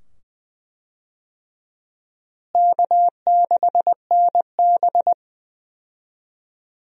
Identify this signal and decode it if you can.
Morse code, heard as a keyed tone, about 21 wpm, 700 Hz. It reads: K6NB
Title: USA Call Signs - 4 Characters - Repeated 2x - 20wpm